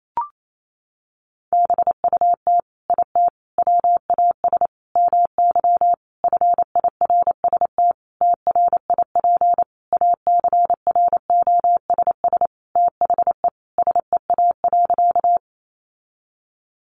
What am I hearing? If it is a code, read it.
Morse code, heard as a keyed tone, about 28 wpm, 700 Hz. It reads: BUT IT WAS MY FIRST TRIP ACROSS THE SEA.
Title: Sentences from Top 1300 Words 28wpm